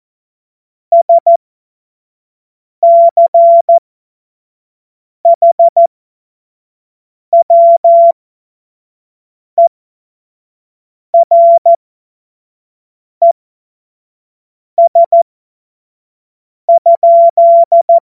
Morse code, heard as a keyed tone, about 14 wpm, spaced as for 5 wpm, 680 Hz. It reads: SCHWERES?
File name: Morsekurs / Witze / Witz 30 Januar 2024 25 F70